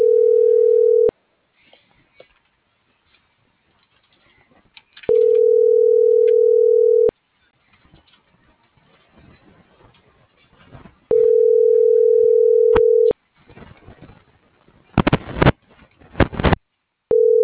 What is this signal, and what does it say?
Ringback tone (North America), 440 + 480 Hz.
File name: CENTRAL GENERAL / out yarit.rdz